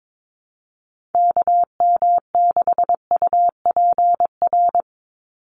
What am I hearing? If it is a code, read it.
Morse code, heard as a keyed tone, about 22 wpm, 700 Hz. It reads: KM6UPR
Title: USA Call Signs 22wpm